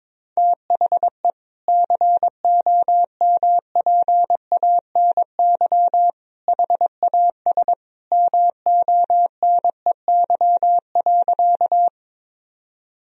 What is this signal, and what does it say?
Morse code, heard as a keyed tone, about 23 wpm, 700 Hz. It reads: THE COMPANY HAS MONEY.
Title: Sentences from Top 1000 Words - Review Part 3 - 22wpm